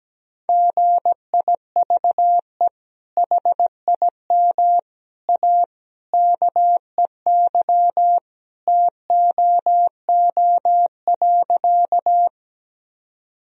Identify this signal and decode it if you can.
Morse code, heard as a keyed tone, about 17 wpm, 700 Hz. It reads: GIVE HIM A KEY TOO.
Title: Sentences from Top 1000 Words - Review Part 4 - 17wpm